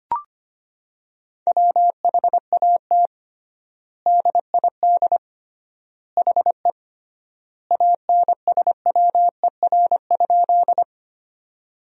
Morse code, heard as a keyed tone, about 25 wpm, 700 Hz. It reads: WHAT DID HE ANSWER?
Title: 3x Word-Spacing - Sentences from Top 300 Words - 25wpm